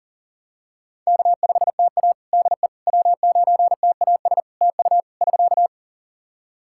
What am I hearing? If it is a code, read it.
Morse code, heard as a keyed tone, about 40 wpm, 700 Hz. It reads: K5TA DE W9TAS TU <SK>
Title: Practice QSO - Exchange 4 - Ending - 40wpm